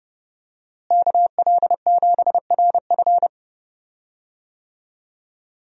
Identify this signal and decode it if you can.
Morse code, heard as a keyed tone, about 30 wpm, 700 Hz. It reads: KL7RF
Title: DXCC-100 Call Signs - 5 Characters - Repeated 2x - 30wpm